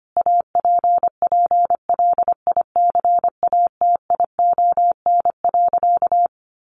Morse code, heard as a keyed tone, about 25 wpm, 700 Hz. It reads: APPLICATION.
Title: Sentences from Top 1200 Words 25wpm